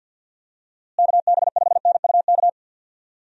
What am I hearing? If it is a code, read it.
Morse code, heard as a keyed tone, about 47 wpm, 700 Hz. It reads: KB5NUK
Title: USA Call Signs - 6 Characters - Repeated 2x - 50wpm